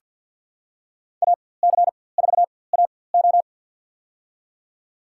Morse code, heard as a keyed tone, about 47 wpm, spaced as for 23 wpm, 700 Hz. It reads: AC4AO
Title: USA Call Signs - 5 Characters - 4x Character Spacing - 50wpm